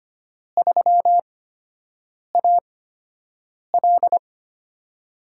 Morse code, heard as a keyed tone, about 25 wpm, spaced as for 7 wpm, 700 Hz. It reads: 3AL
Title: USA Call Signs - 4 Characters - 8x Character Spacing - 25wpm